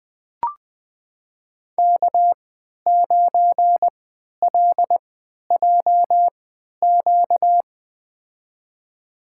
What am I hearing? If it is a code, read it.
Morse code, heard as a keyed tone, about 20 wpm, spaced as for 11 wpm, 700 Hz. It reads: K9LJQ
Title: USA Call Signs - 5 Characters - 3x Character Spacing - 20wpm